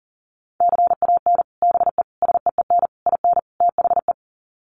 Morse code, heard as a keyed tone, about 37 wpm, 700 Hz. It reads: CAN BE SEEN IN THE
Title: Sets of 5 Words Encore2 40wpm